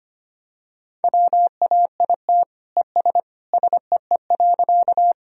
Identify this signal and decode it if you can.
Morse code, heard as a keyed tone, about 25 wpm, 700 Hz. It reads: WAIT ES SEE.
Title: Sentences from Top 500 Words and Abbreviations 25wpm